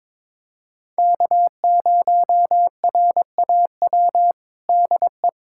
Morse code, heard as a keyed tone, about 22 wpm, 700 Hz. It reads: K0RAW DE
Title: USA Callsign DE Callsign - 5 Characters - 22wpm